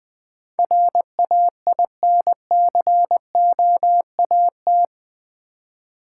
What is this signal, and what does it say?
Morse code, heard as a keyed tone, about 20 wpm, 700 Hz. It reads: RAINCOAT
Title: Easy Words with Hints - 20wpm